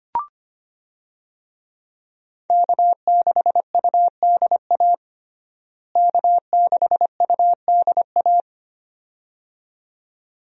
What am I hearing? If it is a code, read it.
Morse code, heard as a keyed tone, about 25 wpm, 700 Hz. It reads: K6UDA K6UDA
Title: USA Call Signs - 5 Characters - Repeated 2x - 25wpm